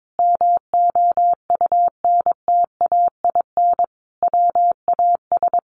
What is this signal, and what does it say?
Morse code, heard as a keyed tone, about 22 wpm, 700 Hz. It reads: MOUNTAIN WAS
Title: Sentences from Top 1300 Words 22wpm